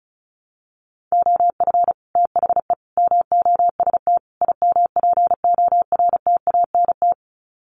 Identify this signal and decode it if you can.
Morse code, heard as a keyed tone, about 34 wpm, 700 Hz. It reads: OF THE MOST IMPORTANT
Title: Sets of 4 Words 35wpm